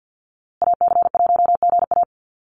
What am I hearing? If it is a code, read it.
Morse code, heard as a keyed tone, about 50 wpm, 700 Hz. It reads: AC1GA
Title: USA Call Signs Repeated 2x 50wpm